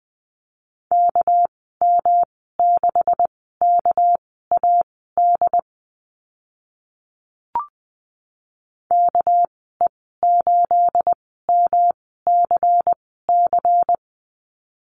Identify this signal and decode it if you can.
Morse code, heard as a keyed tone, about 20 wpm, spaced as for 14 wpm, 700 Hz. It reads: KM6KAD KE8MCC
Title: 219 Lesson 35g - Number 0 - Call Signs 20wpm